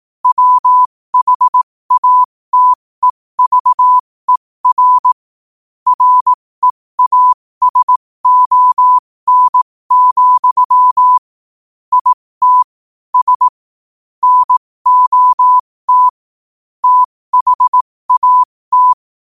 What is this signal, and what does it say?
Morse code, heard as a keyed tone, about 18 wpm, spaced as for 15 wpm, 1000 Hz. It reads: WHATEVER REASON, IT S NOT THAT
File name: qotdfiles / 15wpm